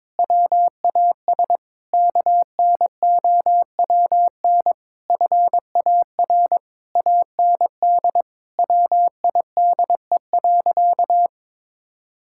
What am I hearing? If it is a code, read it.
Morse code, heard as a keyed tone, about 22 wpm, 700 Hz. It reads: WAS KNOWN FAR AND WIDE.
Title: Sentences from Top 1000 Words - Review Part 6 - 22wpm